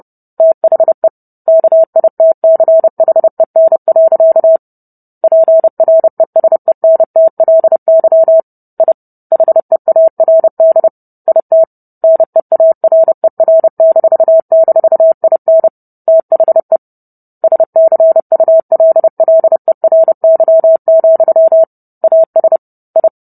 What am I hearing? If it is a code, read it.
Morse code, heard as a keyed tone, about 30 wpm, 650 Hz. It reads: THE KITCHEN. PRESENTLY I HEARD IT NEARER--IN THE SCULLERY, AS I